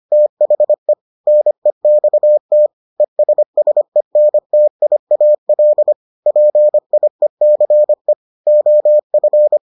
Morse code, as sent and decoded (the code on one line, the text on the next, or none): - .... / -. -..- - / . ... ... -. - .. .- .-.. / .--. .. -.-. / --- ..-.
THE NEXT ESSENTIAL PIECE OF